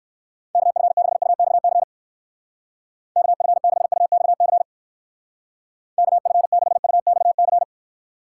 Morse code, heard as a keyed tone, about 59 wpm, 700 Hz. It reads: KF6UXC KF6UXC KF6UXC